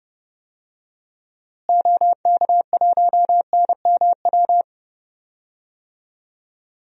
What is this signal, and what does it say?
Morse code, heard as a keyed tone, about 30 wpm, 700 Hz. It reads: OK1NMW
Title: DXCC-100 Call Signs - 6 Characters - Repeated 3x - 30wpm